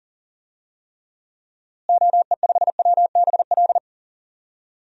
Morse code, heard as a keyed tone, about 40 wpm, 700 Hz. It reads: OE5WBL
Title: DXCC-100 Call Signs - 6 Characters - Repeated 2x - 40wpm